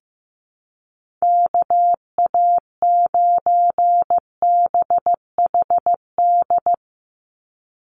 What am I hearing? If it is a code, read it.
Morse code, heard as a keyed tone, about 15 wpm, 700 Hz. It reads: KA9BHD
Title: Call Signs Repeated 2x - 15wpm